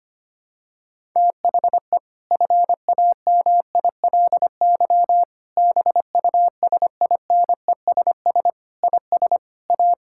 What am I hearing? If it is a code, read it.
Morse code, heard as a keyed tone, about 25 wpm, 700 Hz. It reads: THE FAMILY BUSINESS IS A
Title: Sentences from Top 1000 Words - Review Part 6 - 25wpm